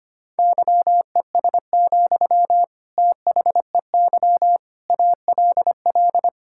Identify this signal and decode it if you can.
Morse code, heard as a keyed tone, about 25 wpm, 700 Hz. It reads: YES, THEY ALL
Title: Sentences from Top 1600 Words 25wpm